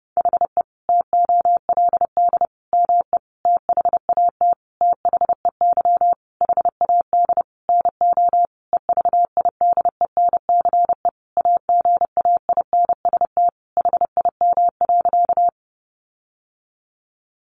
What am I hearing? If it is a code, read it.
Morse code, heard as a keyed tone, about 30 wpm, 700 Hz. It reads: HE TOLD ME THAT THEY HAD NO EVIDENCE AGAINST HIM.
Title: Sentences from Top 1300 Words 30wpm